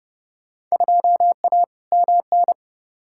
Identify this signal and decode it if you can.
Morse code, heard as a keyed tone, about 30 wpm, 700 Hz. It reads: 2A MN
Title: Field Day Exchanges - 30wpm